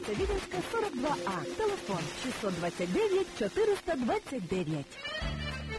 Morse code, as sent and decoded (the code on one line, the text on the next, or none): none